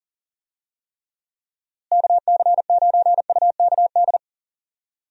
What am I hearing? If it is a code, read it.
Morse code, heard as a keyed tone, about 40 wpm, 700 Hz. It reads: KC9UKD
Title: USA Call Signs - 6 Characters - Repeated 2x - 40wpm